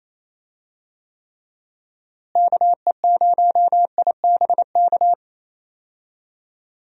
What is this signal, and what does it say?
Morse code, heard as a keyed tone, about 27 wpm, 700 Hz. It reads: KE0IBK
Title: USA Call Signs - 6 Characters - Repeated 2x - 28wpm